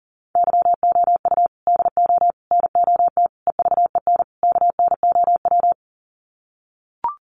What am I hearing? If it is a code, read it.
Morse code, heard as a keyed tone, about 40 wpm, 700 Hz. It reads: YOU DO NOT EVEN KNOW
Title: Sets of 5 Words Encore2 40wpm